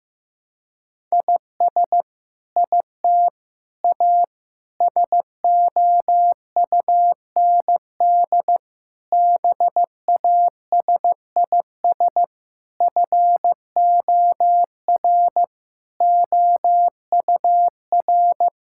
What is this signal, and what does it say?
Morse code, heard as a keyed tone, about 15 wpm, 700 Hz. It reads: IS IT A SOUND BASIS FOR OUR
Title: Sentences from Top 1000 Words - Review Part 5 - 15wpm